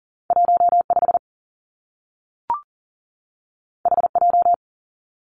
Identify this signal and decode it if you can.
Morse code, heard as a keyed tone, about 40 wpm, 700 Hz. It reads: 1 5 H J
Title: Two Challenging Letter-Number Combinations 40wpm